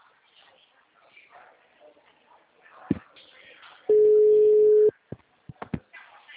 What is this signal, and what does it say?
Ringback tone (Europe, 425 Hz), with speech.
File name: out- -2006